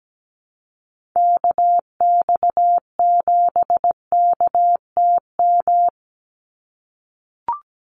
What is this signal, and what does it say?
Morse code, heard as a keyed tone, about 17 wpm, 700 Hz. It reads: KX7KTM
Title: USA Call Signs 17wpm